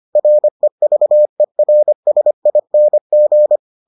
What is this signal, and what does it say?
Morse code, heard as a keyed tone, about 25 wpm, 600 Hz. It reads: REVERSING